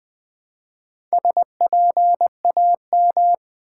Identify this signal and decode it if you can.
Morse code, heard as a keyed tone, about 20 wpm, 700 Hz. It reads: SPAM
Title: Build A Word - 1 Letter At A Time - 3 to 5-Letter Words - 20wpm